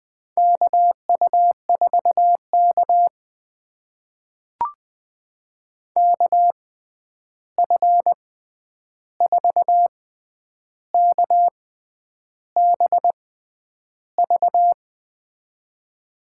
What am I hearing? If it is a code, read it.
Morse code, heard as a keyed tone, about 20 wpm, spaced as for 9 wpm, 700 Hz. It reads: KU4K KF4KBV
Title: USA Call Signs - 6x Character Spacing - 20wpm